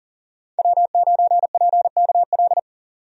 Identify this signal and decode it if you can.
Morse code, heard as a keyed tone, about 40 wpm, 700 Hz. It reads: W9PKL
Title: USA Call Signs - 5 Characters - 40wpm